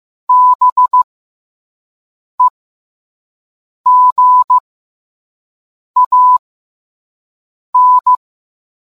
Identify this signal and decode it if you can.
Morse code, heard as a keyed tone, about 15 wpm, spaced as for 5 wpm, 1000 Hz. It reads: BEGAN